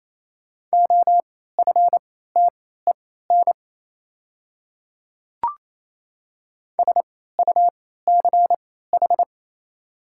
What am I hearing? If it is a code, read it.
Morse code, heard as a keyed tone, about 28 wpm, spaced as for 16 wpm, 700 Hz. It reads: OFTEN SUCH
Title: 3x Character Spacing - Top 500 Words - 28wpm